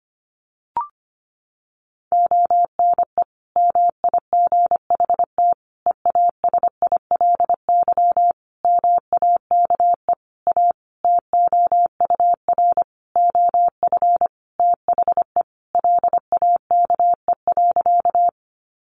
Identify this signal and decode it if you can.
Morse code, heard as a keyed tone, about 25 wpm, 700 Hz. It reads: ONE MIGHT EASILY MAKE A TOUR OF THE LAKE.
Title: Sentences from Top 1400 Words 25wpm